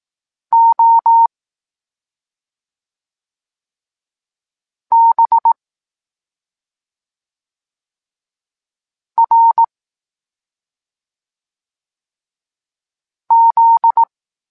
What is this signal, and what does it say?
Morse code, heard as a keyed tone, about 18 wpm, 920 Hz. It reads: O B R Z
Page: Tekst 23.3 – Ciąg dalszy doskonalenia odbioru poznanych liter